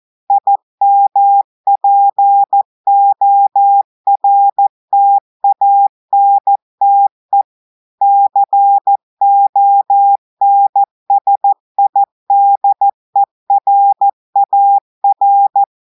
Morse code, heard as a keyed tone, about 14 wpm, 800 Hz. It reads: IMPORTANTE CONSIDERAR
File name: audios / portugais / portugais texte en clair 2 vitesse 14 mots minutes